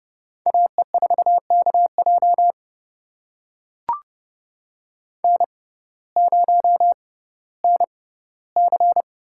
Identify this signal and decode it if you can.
Morse code, heard as a keyed tone, about 30 wpm, spaced as for 17 wpm, 700 Hz. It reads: AE4KJ N0NC
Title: USA Call Signs - 5 Characters - 6x Character Spacing - 30wpm